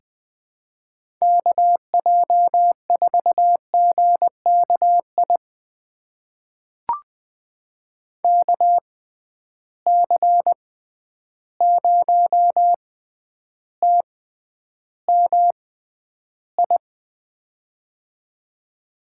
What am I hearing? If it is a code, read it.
Morse code, heard as a keyed tone, about 20 wpm, spaced as for 10 wpm, 700 Hz. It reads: KJ4GKI KC0TMI
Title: USA Call Signs - 6 Characters - 6x Character Spacing - 20wpm